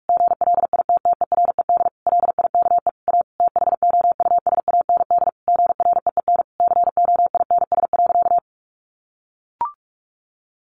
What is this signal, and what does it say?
Morse code, heard as a keyed tone, about 46 wpm, 700 Hz. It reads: GLITTERED LIKE A THOUSAND GREEN COINS.
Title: A Sound Of Thunder 45wpm